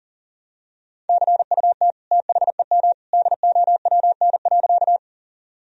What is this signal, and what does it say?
Morse code, heard as a keyed tone, about 40 wpm, 700 Hz. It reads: CUT THEM DOWN.
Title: Sentences from Top 1000 Words - Review Part 1 - 40wpm